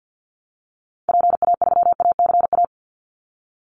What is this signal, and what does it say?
Morse code, heard as a keyed tone, about 48 wpm, 700 Hz. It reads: PA3ACA
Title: Call Signs Repeated 3x - 50wpm